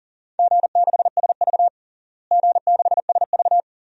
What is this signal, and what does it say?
Morse code, heard as a keyed tone, about 40 wpm, 700 Hz. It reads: G6SV G6SV
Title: DXCC-100 Call Signs - 4 Characters - Repeated 2x - 40wpm